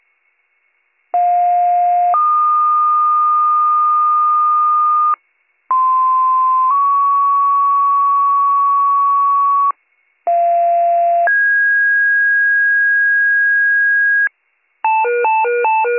Two-tone page, heard as a one-second tone, then a three-second tone, about 710 then 1220 Hz, about 1040 then 1120 Hz, about 690 then 1650 Hz. It is heard from 1.1 to 14.3 s.